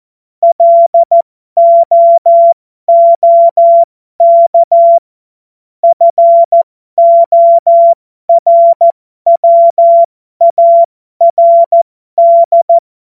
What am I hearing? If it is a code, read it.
Morse code, heard as a keyed tone, about 14 wpm, spaced as for 12 wpm, 680 Hz. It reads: LOOK FORWARD
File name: Morsekurs / Witze